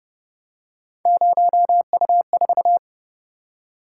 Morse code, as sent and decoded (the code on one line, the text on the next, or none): ----- ..- ....-
0U4